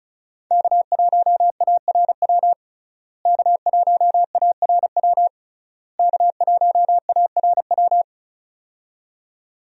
Morse code, heard as a keyed tone, about 34 wpm, 700 Hz. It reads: K1ARW K1ARW K1ARW
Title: USA Call Signs - 5 Characters - Repeated 3x - 35wpm